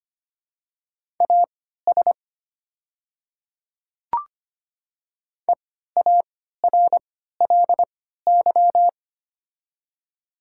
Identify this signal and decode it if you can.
Morse code, heard as a keyed tone, about 25 wpm, spaced as for 14 wpm, 690 Hz. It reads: AS EARLY
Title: 3x Character Spacing - Top 500 Words - 25wpm